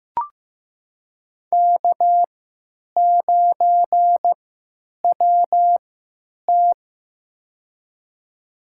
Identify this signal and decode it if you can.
Morse code, heard as a keyed tone, about 15 wpm, spaced as for 9 wpm, 700 Hz. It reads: K9WT
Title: USA Call Signs - 4 Characters - 3x Character Spacing - 15wpm